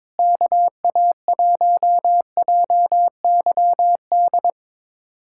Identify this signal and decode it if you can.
Morse code, heard as a keyed tone, about 22 wpm, 700 Hz. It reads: KA1JYD
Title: USA Callsign DE Callsign 22wpm